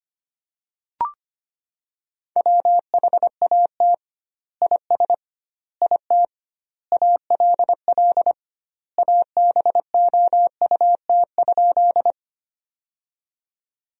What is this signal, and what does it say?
Morse code, heard as a keyed tone, about 25 wpm, 700 Hz. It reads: WHAT IS IT ALL ABOUT?
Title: 2x Word-Spacing - Sentences from Top 100 Words - 25wpm